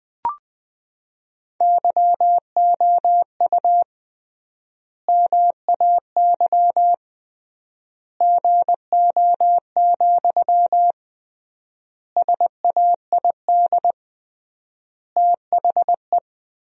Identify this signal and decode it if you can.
Morse code, heard as a keyed tone, about 20 wpm, 700 Hz. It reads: YOU MAY GO, SAID THE
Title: 3x Word-Spacing - Sentences from Top 400 Words - 20wpm